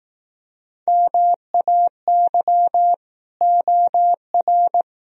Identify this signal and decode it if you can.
Morse code, heard as a keyed tone, about 18 wpm, 700 Hz. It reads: MAY OR
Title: Speed-Racing - Sets of 5 Words 15wpm